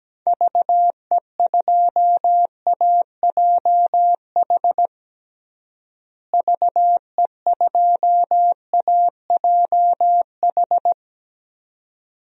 Morse code, heard as a keyed tone, about 17 wpm, 700 Hz. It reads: VE2AJH VE2AJH
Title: DXCC-100 Call Signs - 6 Characters - Repeated 2x - 17wpm